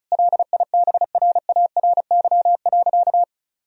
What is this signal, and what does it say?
Morse code, heard as a keyed tone, about 34 wpm, 700 Hz. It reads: LIBRARY.
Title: Sentences from Top 1000 Words - Review Part 4 - 35wpm